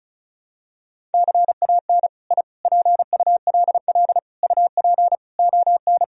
Morse code, heard as a keyed tone, about 36 wpm, 700 Hz. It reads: CAN I PULL UP ON
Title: Sentences from Top 1000 Words - Review Part 2 - 35wpm